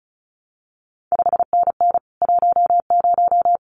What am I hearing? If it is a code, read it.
Morse code, heard as a keyed tone, about 36 wpm, 700 Hz. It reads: 5NN 10
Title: CQ WW DX Contest Exchange Encore - 35wpm